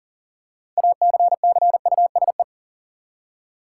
Morse code, heard as a keyed tone, about 40 wpm, 700 Hz. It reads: ACCUSE
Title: Six Letter Words - 40wpm